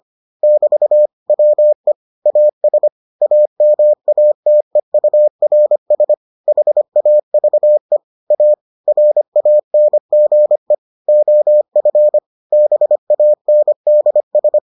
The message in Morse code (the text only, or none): <BT> WE AS AMATEURS HAVE A RANGE OF BANDS